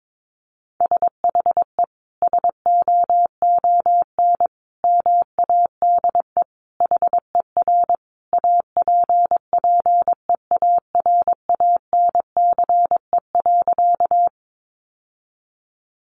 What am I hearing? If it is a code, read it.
Morse code, heard as a keyed tone, about 22 wpm, 700 Hz. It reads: SHE SOON MADE HER APPEARANCE.
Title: Sentences from Top 1100 Words 22wpm